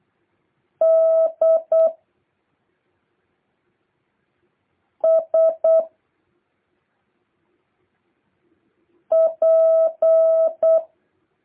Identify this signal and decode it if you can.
Morse code, heard as a keyed tone, about 8 wpm, 650 Hz. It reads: D S P
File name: jh8cbh / 2023cw